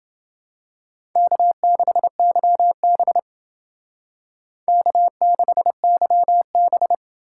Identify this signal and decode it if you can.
Morse code, heard as a keyed tone, about 28 wpm, 690 Hz. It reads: K6YB K6YB